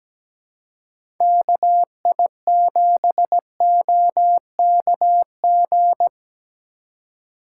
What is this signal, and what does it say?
Morse code, heard as a keyed tone, about 17 wpm, 700 Hz. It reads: KI7OKG
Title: USA Call Signs - 6 Characters - 6x Character Spacing - 17wpm